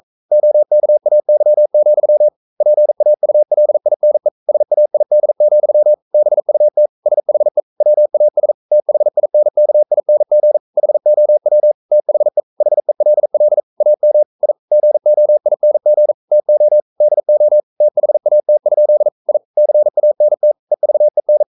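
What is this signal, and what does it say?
Morse code, heard as a keyed tone, about 42 wpm, 600 Hz. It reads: OKAY, PAULINE SAID, BUT SHE WAS THINKING HOW THE HELL AM I GOING TO DO THAT? I CANT EVEN
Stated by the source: Morse code sending of an English novel at 42 wpm.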